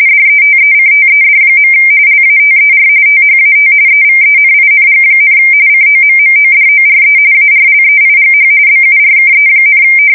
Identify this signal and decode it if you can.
RTTY, about 45.45 baud. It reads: DETECTED. LATER THAT EVENING, COMMODORE ARLEIGH BURKE,
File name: fsk170 / USS-Massachusetts